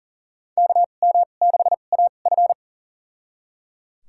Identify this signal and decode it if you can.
Morse code, heard as a keyed tone, about 37 wpm, spaced as for 28 wpm, 690 Hz. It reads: KM6AF
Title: USA Call Signs - 2x Character Spacing - 40wpm